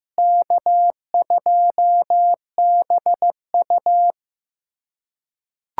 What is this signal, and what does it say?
Morse code, heard as a keyed tone, about 15 wpm, 700 Hz. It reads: K2BU
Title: USA Callsign DE Callsign - 4 Characters - 15wpm